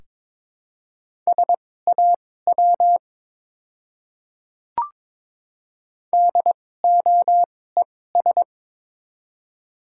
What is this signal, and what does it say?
Morse code, heard as a keyed tone, about 22 wpm, spaced as for 16 wpm, 700 Hz. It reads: SAW DOES
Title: 2x Character Spacing - Top 300 Words - 22wpm